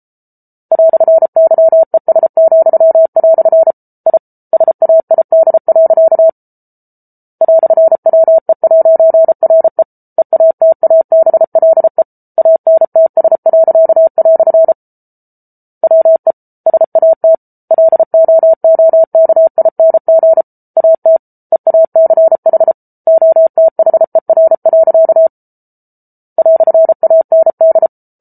Morse code, heard as a keyed tone, about 34 wpm, 650 Hz. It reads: "YES," I SAID. "WE'RE EATABLE ANTS." WE SAT LOOKING AT EACH OTHER. "AND